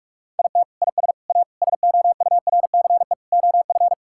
Morse code, heard as a keyed tone, about 46 wpm, 700 Hz. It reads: IT IS A SOURCE OF
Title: Sentences from Top 1000 Words - Review Part 5 - 45wpm